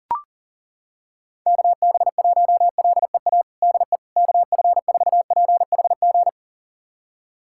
Morse code, heard as a keyed tone, about 40 wpm, 700 Hz. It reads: KB1LEA DE KF4PHG
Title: USA Callsign DE Callsign 40wpm